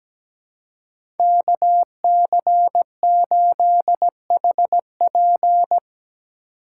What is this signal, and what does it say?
Morse code, heard as a keyed tone, about 17 wpm, 700 Hz. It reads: KC8HP
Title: USA Call Signs - 5 Characters - 3x Character Spacing - 17wpm